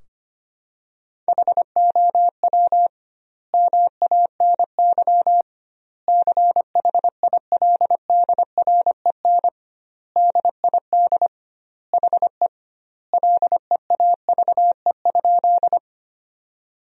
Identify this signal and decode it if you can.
Morse code, heard as a keyed tone, about 25 wpm, 700 Hz. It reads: HOW MANY CHILDREN DID HE LEAVE?
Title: 2x Word-Spacing - Sentences from Top 400 Words - 25wpm